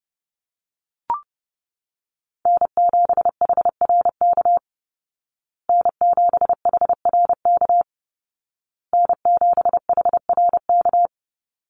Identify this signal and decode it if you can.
Morse code, heard as a keyed tone, about 30 wpm, 700 Hz. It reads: N7HRK N7HRK N7HRK
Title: Call Signs Repeated 2x - 30wpm